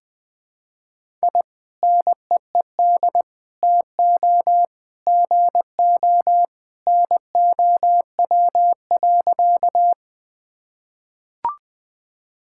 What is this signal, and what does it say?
Morse code, heard as a keyed tone, about 20 wpm, 700 Hz. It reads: I NEED TO GO NOW.
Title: Sentences with 2-Word Phrases - Spoken Beginning - 20wpm